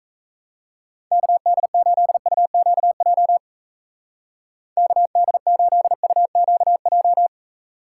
Morse code, heard as a keyed tone, about 36 wpm, 700 Hz. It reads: KD8UQJ KD8UQJ